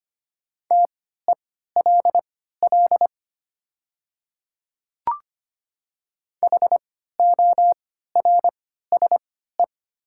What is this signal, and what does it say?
Morse code, heard as a keyed tone, about 25 wpm, spaced as for 14 wpm, 700 Hz. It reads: TELL HORSE
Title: 3x Character Spacing - Top 300 Words - 25wpm